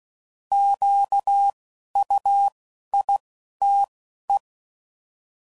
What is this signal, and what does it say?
Morse code, heard as a keyed tone, about 16 wpm, spaced as for 11 wpm, 800 Hz. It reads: QUITE